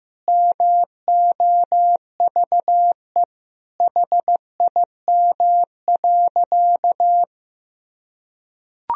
Morse code, heard as a keyed tone, about 15 wpm, 700 Hz. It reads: MOVE HIM.
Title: Sentences from Top 1000 Words - Review Part 1 - 15wpm